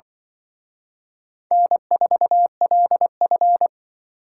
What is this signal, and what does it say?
Morse code, heard as a keyed tone, about 24 wpm, 700 Hz. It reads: N4LF
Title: Speed Racing - USA 1x2 Call Signs 20wpm